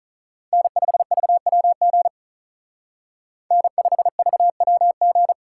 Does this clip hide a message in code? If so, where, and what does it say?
Yes: Morse code from 0.5 to 5.3 s, heard as a keyed tone, about 45 wpm, 700 Hz. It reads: N5VWG N5VWG